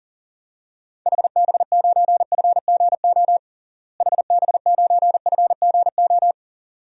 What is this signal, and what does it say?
Morse code, heard as a keyed tone, about 40 wpm, 700 Hz. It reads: HB9FGO HB9FGO
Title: DXCC-100 Call Signs - 6 Characters - Repeated 2x - 40wpm